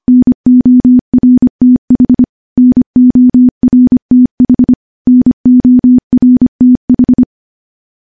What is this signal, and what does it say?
Morse code, heard as a keyed tone, about 25 wpm, 250 Hz. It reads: NORTH NORTH NORTH
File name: morse / morse-training-25WPM